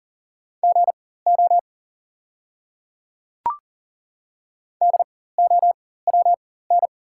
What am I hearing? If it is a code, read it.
Morse code, heard as a keyed tone, about 40 wpm, spaced as for 19 wpm, 690 Hz. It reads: GO DOWN
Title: 4x Character Spacing - Top 100 Words - 40wpm